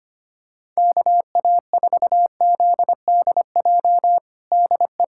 Morse code, heard as a keyed tone, about 25 wpm, 700 Hz. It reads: KA4ZDJ DE
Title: USA Callsign DE Callsign - 6 Characters - 25wpm